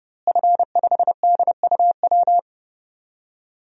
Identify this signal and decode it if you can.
Morse code, heard as a keyed tone, about 30 wpm, 700 Hz. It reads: F5DUW
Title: DXCC-100 Call Signs - 5 Characters - Repeated 2x - 30wpm